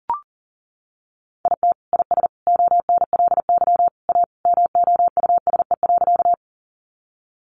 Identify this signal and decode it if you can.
Morse code, heard as a keyed tone, about 40 wpm, 700 Hz. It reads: IT IS ONLY A MOUSE.
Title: Sentences from Top 700 Words 40wpm